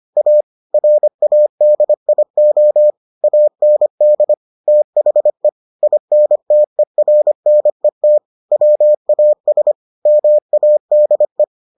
Morse code, heard as a keyed tone, about 25 wpm, 600 Hz. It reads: A RADIO AND THE INTERNET WAS MADE